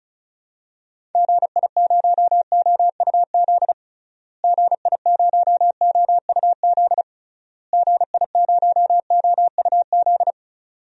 Morse code, heard as a keyed tone, about 36 wpm, 700 Hz. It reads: GI0OUZ GI0OUZ GI0OUZ
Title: DXCC-100 Call Signs - 6 Characters - Repeated 3x - 35wpm